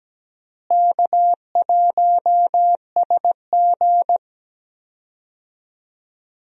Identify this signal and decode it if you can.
Morse code, heard as a keyed tone, about 17 wpm, 700 Hz. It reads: K1SG
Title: USA Call Signs - 4 Characters - Repeated 2x - 17wpm